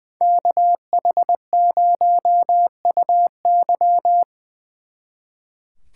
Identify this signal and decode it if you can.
Morse code, heard as a keyed tone, about 24 wpm, 700 Hz. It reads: KH0UY
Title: Speed Racing - USA 2x2 Call Signs 20wpm